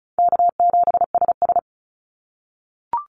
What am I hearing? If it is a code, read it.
Morse code, heard as a keyed tone, about 36 wpm, 700 Hz. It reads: K7SS
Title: Call Signs - 35wpm